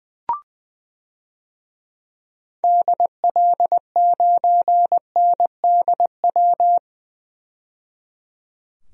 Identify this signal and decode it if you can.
Morse code, heard as a keyed tone, about 20 wpm, 700 Hz. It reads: DL9NDW
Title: DXCC-100 Call Signs - 6 Characters - 20wpm